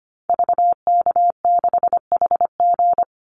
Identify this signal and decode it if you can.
Morse code, heard as a keyed tone, about 25 wpm, 700 Hz. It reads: VK6HG
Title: Call Signs Repeated 2x - 25wpm